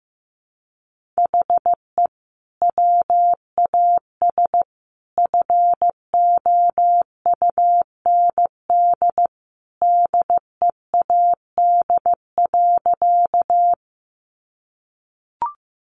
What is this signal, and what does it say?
Morse code, heard as a keyed tone, about 15 wpm, 700 Hz. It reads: HE WAS FOUND DEAD.
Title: Sentences from Top 1500 Words 15wpm